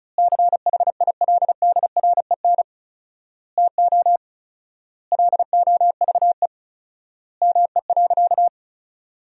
Morse code, heard as a keyed tone, about 36 wpm, 700 Hz. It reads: CHILDREN TO LOVE ME.
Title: 4x Word-Spacing - Sentences from Top 400 Words - 35wpm